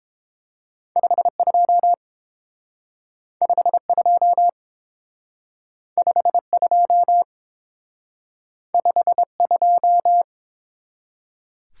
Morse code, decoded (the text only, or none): 52 52 52 52